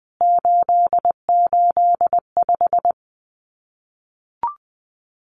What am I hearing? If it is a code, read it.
Morse code, heard as a keyed tone, about 20 wpm, 700 Hz. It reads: 885
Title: CWT Exchange 20wpm